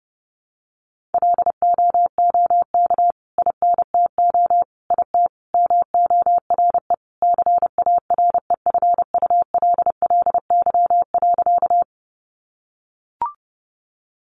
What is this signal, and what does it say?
Morse code, heard as a keyed tone, about 30 wpm, 700 Hz. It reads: LOOK INTO IT MORE CAREFULLY.
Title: Sentences from Top 1300 Words 30wpm